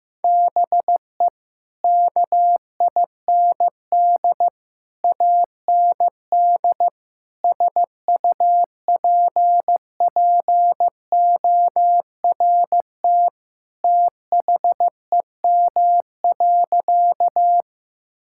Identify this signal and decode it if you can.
Morse code, heard as a keyed tone, about 15 wpm, 700 Hz. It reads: BE KIND AND SUPPORT THEM.
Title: Sentences from Top 1000 Words - Review Part 5 - 15wpm